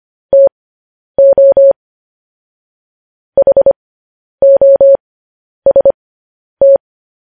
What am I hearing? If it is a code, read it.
Morse code, heard as a keyed tone, about 25 wpm, spaced as for 10 wpm, 550 Hz. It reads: TO HOST
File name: podcast / News headlines-1x-25x10